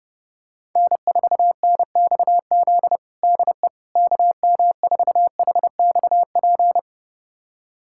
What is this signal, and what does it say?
Morse code, heard as a keyed tone, about 32 wpm, 700 Hz. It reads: N4NXZ DE KM4HXP
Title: USA Callsign DE Callsign 30wpm